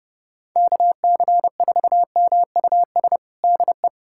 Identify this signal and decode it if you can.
Morse code, heard as a keyed tone, about 30 wpm, 700 Hz. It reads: KC4MUS DE
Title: USA Callsign DE Callsign 30wpm